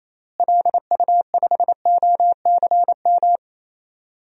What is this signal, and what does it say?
Morse code, heard as a keyed tone, about 28 wpm, 700 Hz. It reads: LU5OCM
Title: DXCC-100 Call Signs - 6 Characters - Repeated 2x - 28wpm